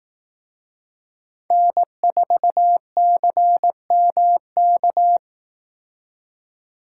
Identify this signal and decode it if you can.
Morse code, heard as a keyed tone, about 19 wpm, 700 Hz. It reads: N4CMK
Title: Speed Racing - USA Call Signs 15wpm